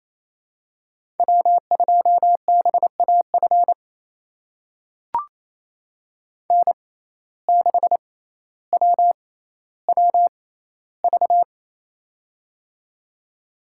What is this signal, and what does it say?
Morse code, heard as a keyed tone, about 28 wpm, spaced as for 14 wpm, 700 Hz. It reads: W2BAF N6WWV
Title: USA Call Signs - 5 Characters - 6x Character Spacing - 28wpm